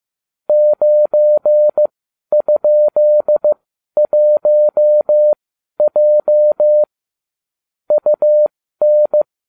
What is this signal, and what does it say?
Morse code, heard as a keyed tone, about 15 wpm, spaced as for 11 wpm, 600 Hz. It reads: 9?1J UN